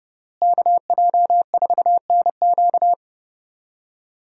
Morse code, heard as a keyed tone, about 30 wpm, 700 Hz. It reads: KJ4NQ
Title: USA Call Signs - 5 Characters - Repeated 3x - 30wpm